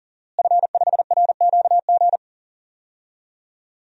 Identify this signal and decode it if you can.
Morse code, heard as a keyed tone, about 40 wpm, 700 Hz. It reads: F5RQG
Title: DXCC-100 Call Signs - 5 Characters - Repeated 3x - 40wpm